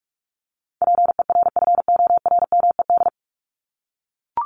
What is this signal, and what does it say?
Morse code, heard as a keyed tone, about 44 wpm, 700 Hz. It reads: PERFORMED
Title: ED Suffix - Top 100 Words with Base Words 45wpm